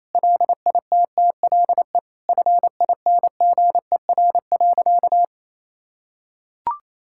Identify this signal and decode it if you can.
Morse code, heard as a keyed tone, about 28 wpm, 700 Hz. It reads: LITTLE FINGER.
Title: Sentences from Top 1600 Words 28wpm